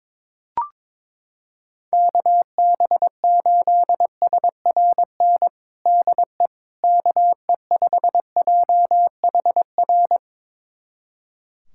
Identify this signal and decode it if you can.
Morse code, heard as a keyed tone, about 22 wpm, 700 Hz. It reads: KB8SRN DE KE5JHR
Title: USA Callsign DE Callsign - 6 Characters - 22wpm